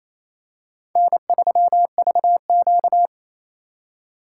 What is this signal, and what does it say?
Morse code, heard as a keyed tone, about 27 wpm, 700 Hz. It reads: N3VQ
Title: USA Call Signs - 4 Characters - 3x Character Spacing - 28wpm